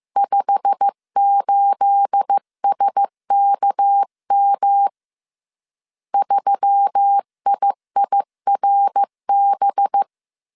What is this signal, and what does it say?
Morse code, heard as a keyed tone, about 15 wpm, 800 Hz. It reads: 58SKM 3IIRB
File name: CW / DET / 1 9 1500C S B 14wpm 3302